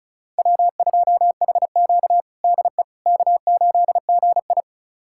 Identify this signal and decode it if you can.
Morse code, heard as a keyed tone, about 36 wpm, 700 Hz. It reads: W2HQ DE K8GI
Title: USA Callsign DE Callsign - 4 Characters - 35wpm